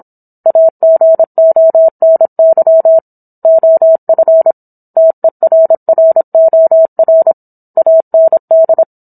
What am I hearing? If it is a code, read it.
Morse code, heard as a keyed tone, about 27 wpm, 650 Hz. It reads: AGONY OF TERROR AND
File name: Book One / 17-Chapter 17 - 26 WPM